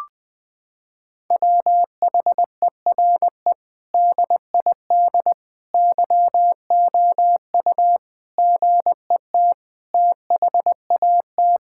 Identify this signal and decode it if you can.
Morse code, heard as a keyed tone, about 20 wpm, 700 Hz. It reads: WHERE DID YOU GET THAT
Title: Sentences from Top 1600 Words 20wpm